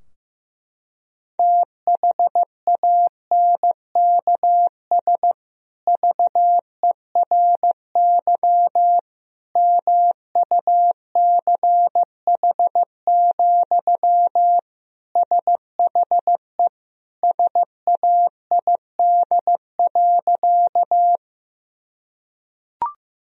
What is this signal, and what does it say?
Morse code, heard as a keyed tone, about 15 wpm, 700 Hz. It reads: THANKS VERY MUCH, SHE SAID.
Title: Sentences from Top 1000 Words - Review Part 3 - 15wpm